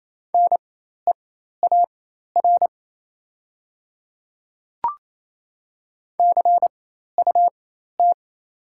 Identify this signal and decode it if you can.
Morse code, heard as a keyed tone, about 29 wpm, spaced as for 13 wpm, 700 Hz. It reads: NEAR CUT
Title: 4x Character Spacing - Top 300 Words - 28wpm